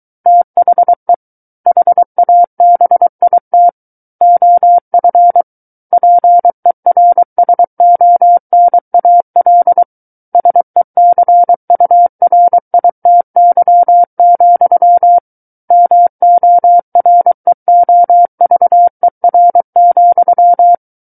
Morse code, heard as a keyed tone, about 23 wpm, 700 Hz. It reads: THE HABIT OF PERSONAL SECURITY, MOREOVER,
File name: Book One / 14-Chapter 14 - 23 WPM